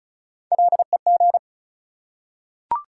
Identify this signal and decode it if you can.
Morse code, heard as a keyed tone, about 36 wpm, 700 Hz. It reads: LEG
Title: Three Letter Words - 35wpm